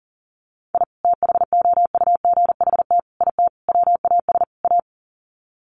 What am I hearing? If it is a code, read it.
Morse code, heard as a keyed tone, about 37 wpm, 700 Hz. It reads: I THOUGHT IT WAS A
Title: Sets of 5 Words 40wpm